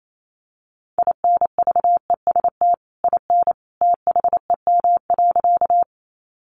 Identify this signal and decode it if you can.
Morse code, heard as a keyed tone, about 28 wpm, 700 Hz. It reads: INVEST IN THEM.
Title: Sentences from Top 1100 Words 28wpm